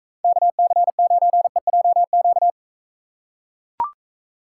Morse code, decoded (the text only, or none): KC9EJQ